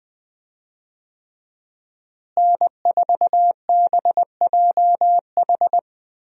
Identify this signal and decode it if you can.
Morse code, heard as a keyed tone, about 21 wpm, 700 Hz. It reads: N4BJH
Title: USA Call Signs - 5 Characters - Repeated 3x - 20wpm